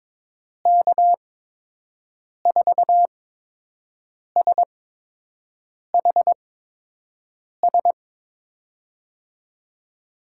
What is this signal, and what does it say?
Morse code, heard as a keyed tone, about 23 wpm, spaced as for 6 wpm, 700 Hz. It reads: K4SHS
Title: USA Call Signs - 6 Characters - 8x Character Spacing - 22wpm